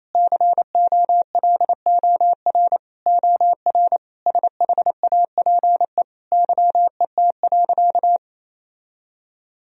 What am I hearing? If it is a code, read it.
Morse code, heard as a keyed tone, about 28 wpm, 700 Hz. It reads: COLOR OR SHAPE YET.
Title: Sentences from Top 1000 Words - Review Part 3 - 28wpm